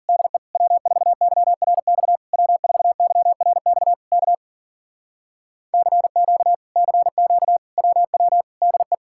Morse code, heard as a keyed tone, about 52 wpm, 700 Hz. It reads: DE W4YRX W4YRX K CQ CQ WW DE